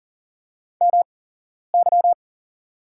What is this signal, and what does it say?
Morse code, heard as a keyed tone, about 40 wpm, spaced as for 11 wpm, 700 Hz. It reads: MY
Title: 8x Character Spacing - Top 300 Words - 40wpm